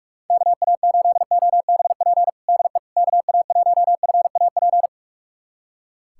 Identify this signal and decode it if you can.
Morse code, heard as a keyed tone, about 42 wpm, 700 Hz. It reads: KA8OBP DE KA1FAP
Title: USA Callsign DE Callsign - 6 Characters - 45wpm